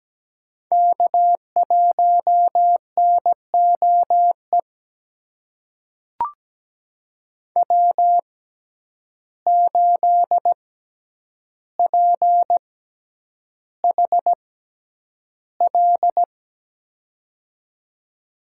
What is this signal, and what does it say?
Morse code, heard as a keyed tone, about 17 wpm, spaced as for 9 wpm, 700 Hz. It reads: K1NOE W8PHL
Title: USA Call Signs - 5 Characters - 6x Character Spacing - 17wpm